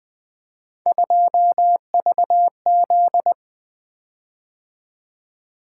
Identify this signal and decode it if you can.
Morse code, heard as a keyed tone, about 20 wpm, 700 Hz. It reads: 2VZ
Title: Three Letter-Number Combinations - 20wpm